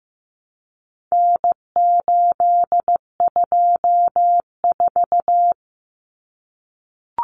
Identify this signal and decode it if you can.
Morse code, heard as a keyed tone, about 15 wpm, 700 Hz. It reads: N824
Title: Common Cut Numbers - 4-Digit 15wpm